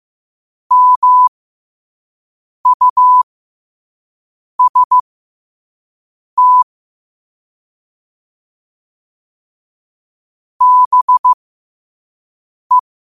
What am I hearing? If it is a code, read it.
Morse code, heard as a keyed tone, about 15 wpm, spaced as for 5 wpm, 1000 Hz. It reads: MUST BE